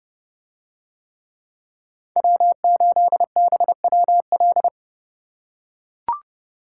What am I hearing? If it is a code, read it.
Morse code, heard as a keyed tone, about 29 wpm, 700 Hz. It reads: W8BWL